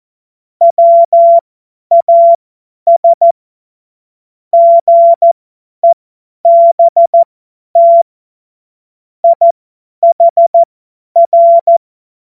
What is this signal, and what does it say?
Morse code, heard as a keyed tone, about 14 wpm, spaced as for 10 wpm, 680 Hz. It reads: WAS GEBT IHR
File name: Morsekurs / Witze / Witz 26 März 2024 50 F70